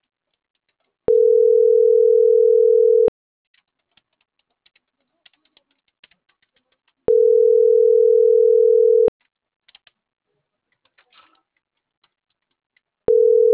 Ringback tone (North America), 440 + 480 Hz.